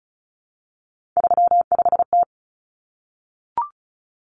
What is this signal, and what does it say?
Morse code, heard as a keyed tone, about 34 wpm, 700 Hz. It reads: 35T
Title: Common Cut Numbers - 3-Digit 35wpm